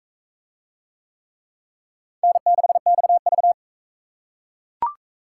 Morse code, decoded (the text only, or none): N6XV